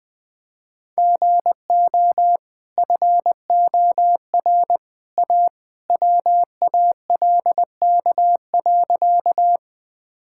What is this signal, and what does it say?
Morse code, heard as a keyed tone, about 20 wpm, 700 Hz. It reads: GO FOR A WALK.
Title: Easy Sentences with Hints - 20wpm